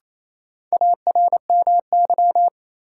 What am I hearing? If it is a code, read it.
Morse code, heard as a keyed tone, about 28 wpm, 700 Hz. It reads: ARMY
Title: Sentences from Top 800 Words - New Word Drill - 28wpm